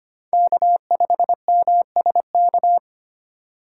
Morse code, heard as a keyed tone, about 31 wpm, 700 Hz. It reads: K5MSK
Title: Speed Racing - USA Call Signs 25wpm